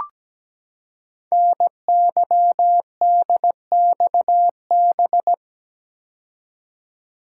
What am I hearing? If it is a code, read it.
Morse code, heard as a keyed tone, about 17 wpm, 700 Hz. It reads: NYDXB
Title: Code Groups - 5 Letters - 17wpm